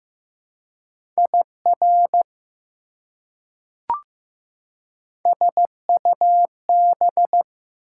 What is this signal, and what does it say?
Morse code, heard as a keyed tone, about 15 wpm, 700 Hz. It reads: IR SUB
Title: Prefixes and Suffixes 15wpm